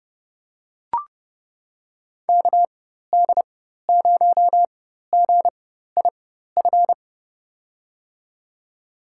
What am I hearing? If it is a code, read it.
Morse code, heard as a keyed tone, about 31 wpm, spaced as for 14 wpm, 700 Hz. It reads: KD0GIF
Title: USA Call Signs - 6 Characters - 4x Character Spacing - 30wpm